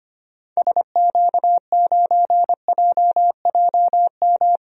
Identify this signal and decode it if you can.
Morse code, heard as a keyed tone, about 25 wpm, 700 Hz. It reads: SQ9JJM
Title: DXCC-100 Call Signs - 6 Characters - 25wpm